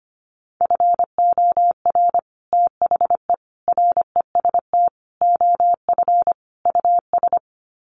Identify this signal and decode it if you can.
Morse code, heard as a keyed tone, about 25 wpm, 700 Hz. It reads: FOR THE REST OF US
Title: Sets of 5 Words Encore 25wpm